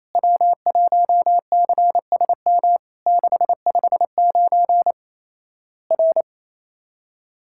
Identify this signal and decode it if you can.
Morse code, heard as a keyed tone, about 28 wpm, 700 Hz. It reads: W1CSM 659 R
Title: Worked All Europe DX Contest - QTC Exchange - 28wpm